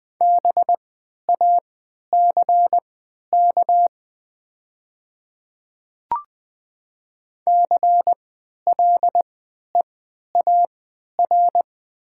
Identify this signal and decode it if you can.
Morse code, heard as a keyed tone, about 20 wpm, spaced as for 11 wpm, 700 Hz. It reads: BACK CLEAR
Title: 3x Character Spacing - Top 500 Words - 20wpm